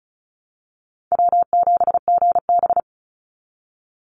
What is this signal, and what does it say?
Morse code, heard as a keyed tone, about 34 wpm, 700 Hz. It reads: W7GB
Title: Call Signs - 35wpm